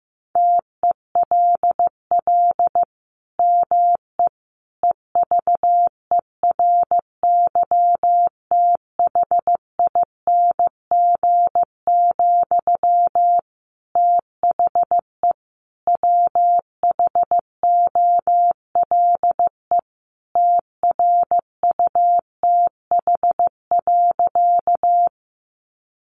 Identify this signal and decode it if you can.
Morse code, heard as a keyed tone, about 15 wpm, 700 Hz. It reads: TELL ME EVERYTHING, THE WHOLE TRUTH.
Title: Sentences from Top 800 Words 15wpm